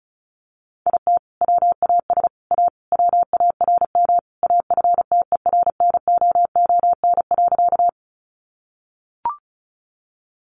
Morse code, heard as a keyed tone, about 34 wpm, 700 Hz. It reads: IT WAS A WARM AFTERNOON.
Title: Sentences from Top 1500 Words 35wpm